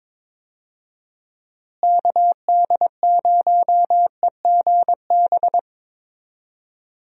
Morse code, heard as a keyed tone, about 22 wpm, 700 Hz. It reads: KD0EGB
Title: USA Call Signs - 6 Characters - Repeated 2x - 22wpm